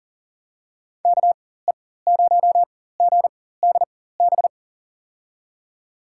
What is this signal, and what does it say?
Morse code, heard as a keyed tone, about 40 wpm, spaced as for 19 wpm, 700 Hz. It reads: KE0GDB
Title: USA Call Signs - 4x Character Spacing - 40wpm